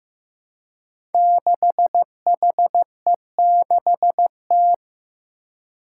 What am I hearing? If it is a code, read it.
Morse code, heard as a keyed tone, about 16 wpm, 700 Hz. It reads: 6HE6T
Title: Five Challenging Letter-Number Combinations 15wpm